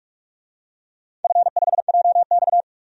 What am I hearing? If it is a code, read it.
Morse code, heard as a keyed tone, about 46 wpm, 700 Hz. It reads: F5JX
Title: DXCC-100 Call Signs - 4 Characters - 45wpm